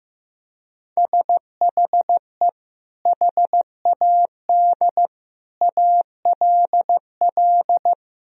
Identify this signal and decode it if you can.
Morse code, heard as a keyed tone, about 14 wpm, 700 Hz. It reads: SHE HAD ALL
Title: Sentences from Top 1000 Words - Review Part 1 - 15wpm